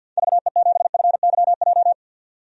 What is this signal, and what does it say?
Morse code, heard as a keyed tone, about 48 wpm, 700 Hz. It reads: VE7FYJ
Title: DXCC-100 Call Signs - 6 Characters - 50wpm